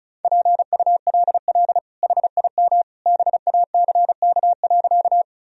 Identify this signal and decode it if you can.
Morse code, heard as a keyed tone, about 36 wpm, 700 Hz. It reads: PULL HIM BACK.
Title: Sentences from Top 1000 Words - Review Part 2 - 35wpm